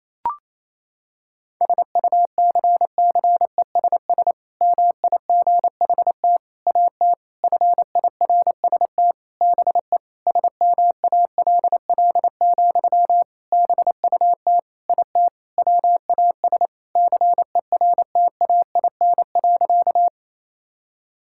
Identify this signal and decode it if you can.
Morse code, heard as a keyed tone, about 28 wpm, 700 Hz. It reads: SUCCESS MIGHT AT FIRST BE SMALL, BUT IT WAS CERTAIN.
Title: Sentences from Top 1000 Words - Review Part 6 - 28wpm